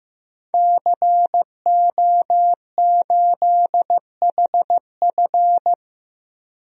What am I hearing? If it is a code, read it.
Morse code, heard as a keyed tone, about 16 wpm, 700 Hz. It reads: CO8HF
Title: DXCC-100 Call Signs - 5 Characters - 15wpm